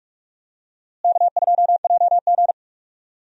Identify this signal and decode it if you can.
Morse code, heard as a keyed tone, about 48 wpm, 700 Hz. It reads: K2JG
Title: USA Call Signs - 4 Characters - Repeated 2x - 45wpm